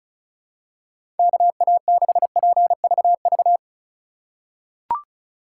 Morse code, heard as a keyed tone, about 36 wpm, 700 Hz. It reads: KA6PVV
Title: USA Call Signs - 3x Character Spacing - 35wpm